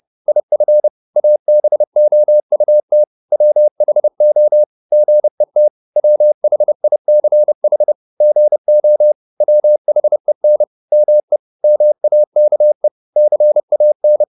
Morse code, heard as a keyed tone, about 30 wpm, 600 Hz. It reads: IF ABOUT WHO GET WHICH GO WHEN ME MAKE CAN
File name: Top-100-Words-30